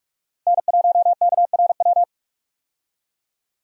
Morse code, heard as a keyed tone, about 47 wpm, 700 Hz. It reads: N1KRW
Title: USA Call Signs - 5 Characters - 45wpm